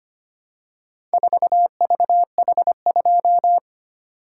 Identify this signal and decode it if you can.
Morse code, heard as a keyed tone, about 25 wpm, 700 Hz. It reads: 4VH2
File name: practice / 25wpm / Four Letter-Number Combinations - 25wpm